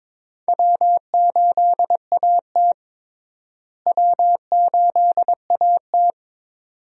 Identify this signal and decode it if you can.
Morse code, heard as a keyed tone, about 21 wpm, 700 Hz. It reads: W8AT W8AT
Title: USA Call Signs - 4 Characters - Repeated 2x - 22wpm